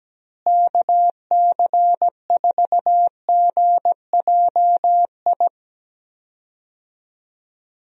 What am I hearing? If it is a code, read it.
Morse code, heard as a keyed tone, about 17 wpm, 700 Hz. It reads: KC4GJI
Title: USA Call Signs - 6 Characters - Repeated 2x - 17wpm